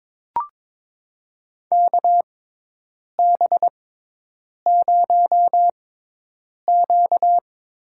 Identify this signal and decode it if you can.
Morse code, heard as a keyed tone, about 22 wpm, spaced as for 8 wpm, 700 Hz. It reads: KB0Q
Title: USA Call Signs - 6 Characters - 6x Character Spacing - 22wpm